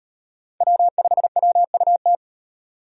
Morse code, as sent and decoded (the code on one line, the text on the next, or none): .-- ..... .-- ..- -
W5WUT